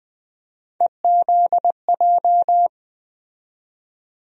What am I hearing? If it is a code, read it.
Morse code, heard as a keyed tone, about 20 wpm, 700 Hz. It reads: EZJ
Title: USA Call Sign Postfixes - 3 Letters 20wpm